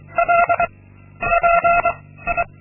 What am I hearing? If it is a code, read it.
Morse code, heard as a keyed tone, about 23 wpm, 690 Hz, with noise. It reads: L P I
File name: Linksmakalnis-2